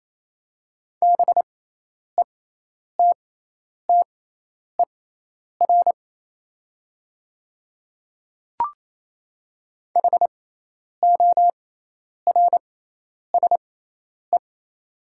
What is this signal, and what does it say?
Morse code, heard as a keyed tone, about 28 wpm, spaced as for 10 wpm, 700 Hz. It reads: BETTER HORSE
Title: 6x Character Spacing - Top 500 Words - 28wpm